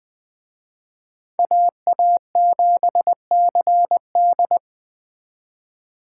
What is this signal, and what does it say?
Morse code, heard as a keyed tone, about 21 wpm, 700 Hz. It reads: AA7CD
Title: Speed Racing - USA 2x2 Call Signs 17wpm